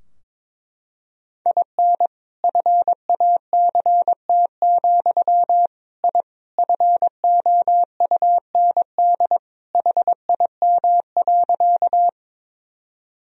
Morse code, heard as a keyed tone, about 22 wpm, 700 Hz. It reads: IN FACT, I FOUND HIM.
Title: Sentences from Top 1000 Words - Review Part 3 - 22wpm